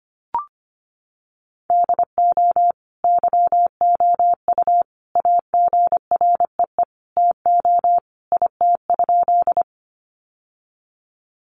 Morse code, heard as a keyed tone, about 25 wpm, 700 Hz. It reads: DO YOU AGREE TO IT?
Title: Sentences from Top 1100 Words 25wpm